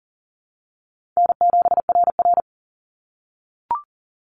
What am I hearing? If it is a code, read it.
Morse code, heard as a keyed tone, about 38 wpm, 700 Hz. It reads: N7RR
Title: NA SOTA Activator Callsigns 40wpm